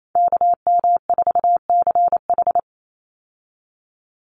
Morse code, heard as a keyed tone, about 28 wpm, 700 Hz. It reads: KM4CH
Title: Call Signs Repeated 2x - 28wpm